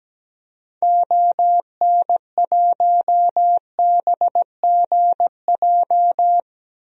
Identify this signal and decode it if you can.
Morse code, heard as a keyed tone, about 17 wpm, 700 Hz. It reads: ON1BGJ
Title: DXCC-100 Call Signs - 6 Characters - Repeated 2x - 17wpm